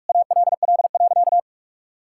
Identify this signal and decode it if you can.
Morse code, heard as a keyed tone, about 41 wpm, 700 Hz. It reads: ALL.
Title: Sentences from Top 1600 Words 45wpm